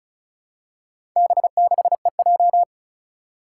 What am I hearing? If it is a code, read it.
Morse code, heard as a keyed tone, about 36 wpm, 700 Hz. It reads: B6EJ
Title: Four Challenging Letter-Number Combinations 35wpm